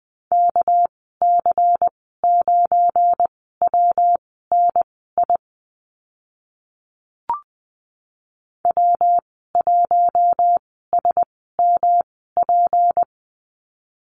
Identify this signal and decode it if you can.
Morse code, heard as a keyed tone, about 20 wpm, spaced as for 15 wpm, 700 Hz. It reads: KC9WNI W1SMP
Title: 238 Lesson 38g - Letter q - Call Signs 20wpm